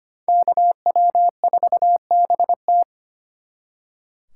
Morse code, heard as a keyed tone, about 33 wpm, 700 Hz. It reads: KW4BT